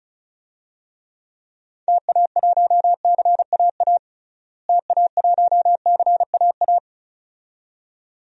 Morse code, heard as a keyed tone, about 36 wpm, 700 Hz. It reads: TA1CAA TA1CAA
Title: DXCC-100 Call Signs - 6 Characters - Repeated 2x - 35wpm